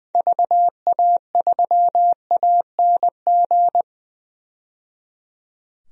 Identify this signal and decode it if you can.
Morse code, heard as a keyed tone, about 21 wpm, 700 Hz. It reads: VA3ANG
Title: DXCC-100 Call Signs - 6 Characters - 20wpm